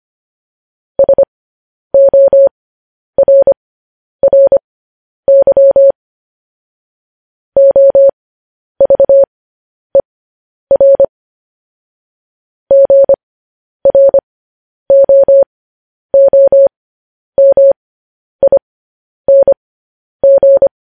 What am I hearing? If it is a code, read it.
Morse code, heard as a keyed tone, about 25 wpm, spaced as for 10 wpm, 550 Hz. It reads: SORRY OVER GROOMING
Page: News Headlines in Morse Code at 10 WPM